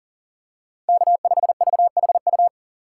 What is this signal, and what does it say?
Morse code, heard as a keyed tone, about 40 wpm, 700 Hz. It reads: K5VHU
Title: USA Call Signs - 5 Characters - 6x Character Spacing - 40wpm